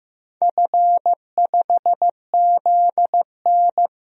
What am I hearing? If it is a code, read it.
Morse code, heard as a keyed tone, about 15 wpm, 700 Hz. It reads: F5ZN
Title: DXCC-100 Call Signs - 4 Characters - 15wpm